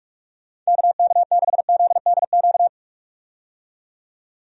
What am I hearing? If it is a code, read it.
Morse code, heard as a keyed tone, about 46 wpm, 700 Hz. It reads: KK6ZDQ
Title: USA Call Signs - 6 Characters - 45wpm